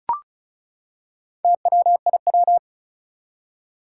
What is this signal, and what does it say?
Morse code, heard as a keyed tone, about 36 wpm, 700 Hz. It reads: TWIW
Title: Four Challenging Letter-Number Combinations 35wpm